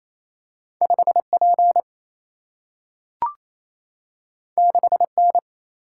Morse code, heard as a keyed tone, about 28 wpm, 700 Hz. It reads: 5P 6N
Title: Single-Letter Single-Number 28wpm